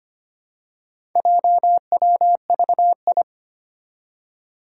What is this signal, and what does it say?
Morse code, heard as a keyed tone, about 25 wpm, 700 Hz. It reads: JWVI
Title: Four Challenging Letter-Number Combinations 25wpm